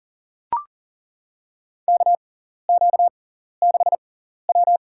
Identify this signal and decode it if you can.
Morse code, heard as a keyed tone, about 40 wpm, spaced as for 14 wpm, 700 Hz. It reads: KQ6W
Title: USA Call Signs - 4 Characters - 6x Character Spacing - 40wpm